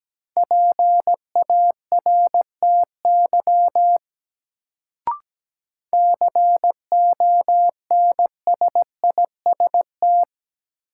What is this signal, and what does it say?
Morse code, heard as a keyed tone, about 17 wpm, 700 Hz. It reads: PARTY CONSIST
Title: Sentences from Top 1400 Words - New Word Drill - 17wpm